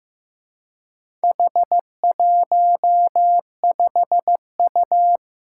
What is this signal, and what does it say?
Morse code, heard as a keyed tone, about 15 wpm, 700 Hz. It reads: H15U
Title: Four Challenging Letter-Number Combinations 15wpm